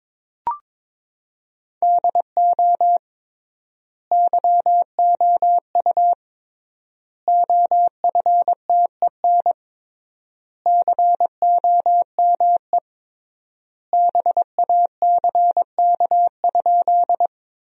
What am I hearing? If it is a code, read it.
Morse code, heard as a keyed tone, about 22 wpm, 700 Hz. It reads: DO YOU OFTEN COME BACK?
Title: 3x Word-Spacing - Sentences from Top 300 Words - 22wpm